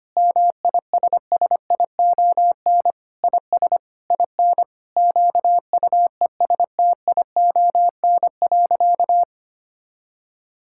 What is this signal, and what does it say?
Morse code, heard as a keyed tone, about 25 wpm, 700 Hz. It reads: MISSION IS IN QUESTION.
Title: Sentences from Top 1600 Words 25wpm